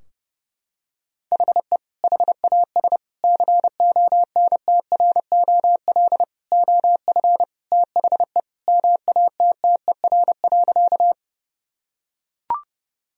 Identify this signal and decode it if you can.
Morse code, heard as a keyed tone, about 30 wpm, 700 Hz. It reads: HE HAS CONTROL OF THE MATTER.
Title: Sentences from Top 1000 Words - Review Part 3 - 30wpm